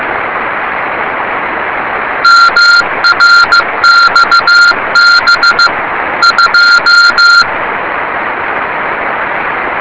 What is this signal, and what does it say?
Morse code, heard as a keyed tone, about 15 wpm, 1390 Hz. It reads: MRXB 2
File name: rs30 cw